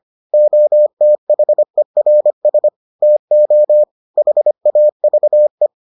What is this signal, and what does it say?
Morse code, heard as a keyed tone, about 25 wpm, 600 Hz. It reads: OTHERS TO HAVE